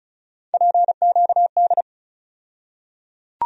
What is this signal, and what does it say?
Morse code, heard as a keyed tone, about 34 wpm, 700 Hz. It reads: PQD
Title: USA Call Sign Postfixes - 3 Letters 35wpm